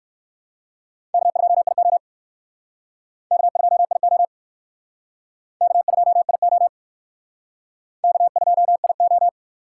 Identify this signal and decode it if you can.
Morse code, heard as a keyed tone, about 60 wpm, 700 Hz. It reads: K2IO K2IO K2IO K2IO